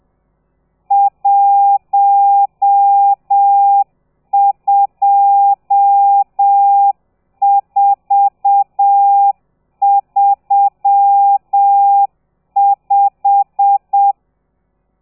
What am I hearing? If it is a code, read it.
Morse code, heard as a keyed tone, about 7 wpm, 800 Hz. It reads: 12435
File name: telegrafia / i2tfj / Lez12 30